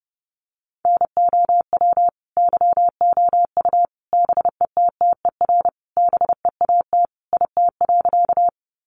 Morse code, heard as a keyed tone, about 30 wpm, 700 Hz. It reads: NOW YOU BETTER BEAT IT.
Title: Sentences from Top 1300 Words 30wpm